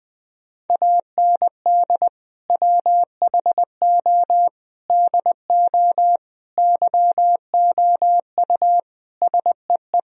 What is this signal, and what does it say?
Morse code, heard as a keyed tone, about 20 wpm, 700 Hz. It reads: AND WHO DO YOU SEE
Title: Sentences from Top 1000 Words - Review Part 3 - 20wpm